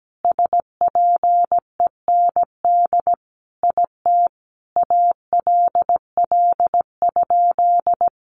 Morse code, heard as a keyed tone, about 17 wpm, 700 Hz. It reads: SPEND IT ALL?
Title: Sentences from Top 800 Words 17wpm